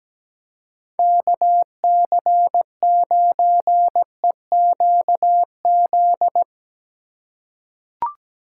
Morse code, heard as a keyed tone, about 17 wpm, 700 Hz. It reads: KC9EQZ
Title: USA Call Signs - 3x Character Spacing - 17wpm